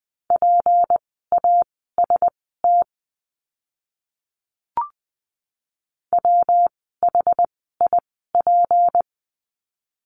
Morse code, heard as a keyed tone, about 20 wpm, spaced as for 14 wpm, 700 Hz. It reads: PAST WHIP
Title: 153 Lesson 25f - Letter p - Words 20wpm